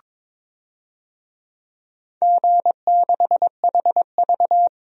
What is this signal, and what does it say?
Morse code, heard as a keyed tone, about 22 wpm, 700 Hz. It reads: G6HV
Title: DXCC-100 Call Signs - 4 Characters - 22wpm